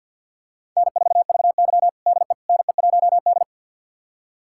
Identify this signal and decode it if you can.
Morse code, heard as a keyed tone, about 52 wpm, 700 Hz. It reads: N4VY DE NE1D
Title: USA Callsign DE Callsign - 4 Characters - 50wpm